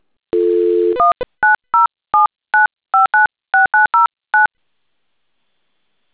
DTMF, heard as telephone touch tones, tones of about 150 ms. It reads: 19079596909